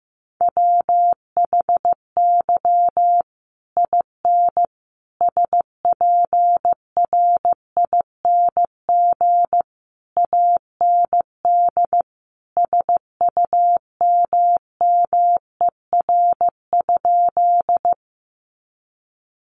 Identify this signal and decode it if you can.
Morse code, heard as a keyed tone, about 15 wpm, 700 Hz. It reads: WHY IN SPRING AND SUMMER?
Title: Sentences from Top 1200 Words 15wpm